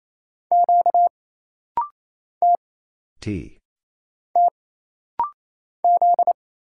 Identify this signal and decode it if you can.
Morse code, heard as a keyed tone, about 27 wpm, 700 Hz. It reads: Q T T Z
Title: Single Letter-Number Alt - Rapid-Fire 28wpm